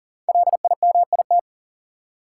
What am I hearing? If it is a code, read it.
Morse code, heard as a keyed tone, about 40 wpm, 700 Hz. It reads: LIMIT
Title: Sentences from Top 1500 Words - New Word Drill Alt - 40wpm